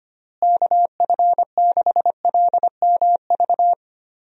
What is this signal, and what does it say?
Morse code, heard as a keyed tone, about 25 wpm, 700 Hz. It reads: KF6LMV
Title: USA Call Signs - 6 Characters - Repeated 3x - 25wpm